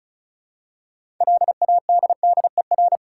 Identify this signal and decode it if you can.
Morse code, heard as a keyed tone, about 36 wpm, 700 Hz. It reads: LADDER
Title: Six Letter Words - 35wpm